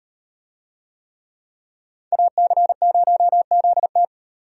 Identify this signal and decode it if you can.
Morse code, heard as a keyed tone, about 38 wpm, 700 Hz. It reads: AC0ZT